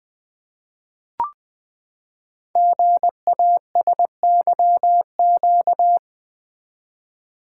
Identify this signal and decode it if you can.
Morse code, heard as a keyed tone, about 20 wpm, 700 Hz. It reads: GASYQ
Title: Code Groups - 5 Letter:Numbers - 20wpm